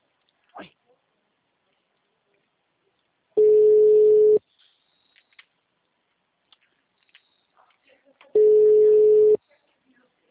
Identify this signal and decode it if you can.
Ringback tone (Europe, 425 Hz), with speech.
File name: out- -2004- - - .16